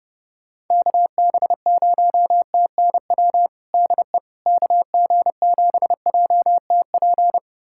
Morse code, heard as a keyed tone, about 32 wpm, 700 Hz. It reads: KB0TNW DE KG7JTP
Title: USA Callsign DE Callsign - 6 Characters - 30wpm